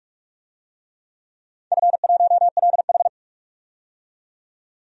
Morse code, heard as a keyed tone, about 42 wpm, 700 Hz. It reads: F1LH
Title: DXCC-100 Call Signs - 4 Characters - 45wpm